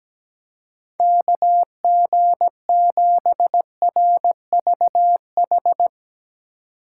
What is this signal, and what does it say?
Morse code, heard as a keyed tone, about 17 wpm, 700 Hz. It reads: KG7RVH
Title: USA Call Signs - 6 Characters - 2x Character Spacing - 17wpm